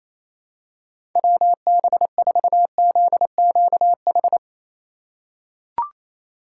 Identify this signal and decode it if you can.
Morse code, heard as a keyed tone, about 28 wpm, 700 Hz. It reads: WB4ZQH
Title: USA Call Signs - 6 Characters - 8x Character Spacing - 28wpm